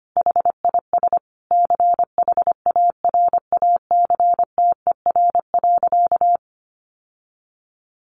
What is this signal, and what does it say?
Morse code, heard as a keyed tone, about 25 wpm, 700 Hz. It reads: HIS CHARACTER.
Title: Sentences from Top 1200 Words 25wpm